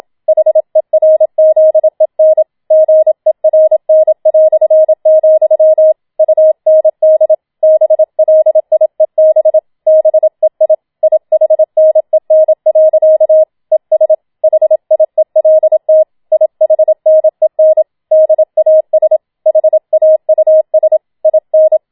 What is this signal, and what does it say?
Morse code, heard as a keyed tone, about 27 wpm, 630 Hz. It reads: HERZEN GERN", UND BLIEB BEI IHNEN. ES HIELT IHNEN DAS HAUS IN
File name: Schneewittchen Teil2 27wpm